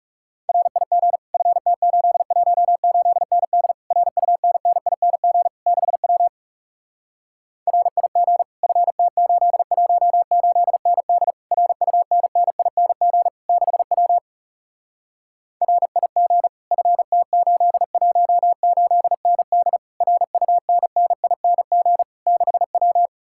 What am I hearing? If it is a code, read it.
Morse code, heard as a keyed tone, about 40 wpm, 700 Hz. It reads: RIG FT818ND RUNNING 6W RIG FT818ND RUNNING 6W RIG FT818ND RUNNING 6W